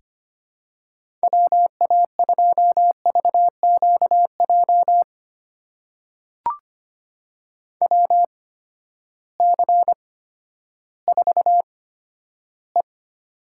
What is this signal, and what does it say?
Morse code, heard as a keyed tone, about 25 wpm, spaced as for 12 wpm, 700 Hz. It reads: WA2VQJ WC4E
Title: USA Call Signs - 8x Character Spacing - 25wpm